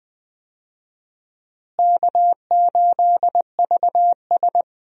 Morse code, heard as a keyed tone, about 20 wpm, 700 Hz. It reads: K8VS
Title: USA Call Signs - 4 Characters - Repeated 3x - 20wpm